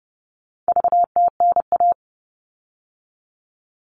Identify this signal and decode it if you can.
Morse code, heard as a keyed tone, about 30 wpm, 700 Hz. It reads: VTNA
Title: All Cut Numbers - 4-Digit 30wpm